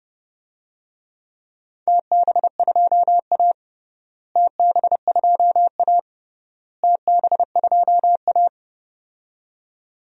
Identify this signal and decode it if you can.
Morse code, heard as a keyed tone, about 30 wpm, 700 Hz. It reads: TB2A TB2A TB2A
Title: DXCC-100 Call Signs - 4 Characters - Repeated 3x - 30wpm